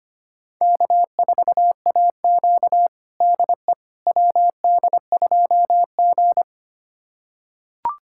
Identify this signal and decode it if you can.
Morse code, heard as a keyed tone, about 25 wpm, 700 Hz. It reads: K4AQ DE WD2G
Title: USA Callsign DE Callsign - 4 Characters - 25wpm